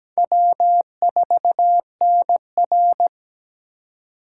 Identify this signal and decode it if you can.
Morse code, heard as a keyed tone, about 17 wpm, 700 Hz. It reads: W4NR
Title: USA Call Signs - 4 Characters - 4x Character Spacing - 17wpm